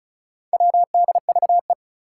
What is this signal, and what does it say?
Morse code, heard as a keyed tone, about 36 wpm, 700 Hz. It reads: WDVE
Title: Four Challenging Letter-Number Combinations 35wpm